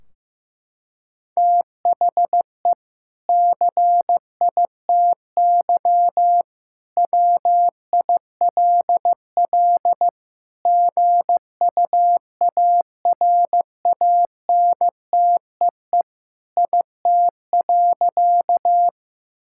Morse code, heard as a keyed tone, about 15 wpm, 700 Hz. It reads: THE CITY WILL GUARANTEE IT.
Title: Sentences from Top 1600 Words 15wpm